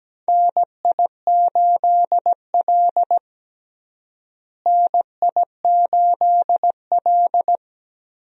Morse code, heard as a keyed tone, about 17 wpm, 700 Hz. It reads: NI8L NI8L
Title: USA Call Signs - 4 Characters - Repeated 3x - 17wpm